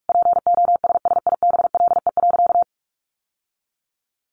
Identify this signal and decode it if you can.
Morse code, heard as a keyed tone, about 44 wpm, 700 Hz. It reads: POSSIBLE.
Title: Sentences from Top 1300 Words 45wpm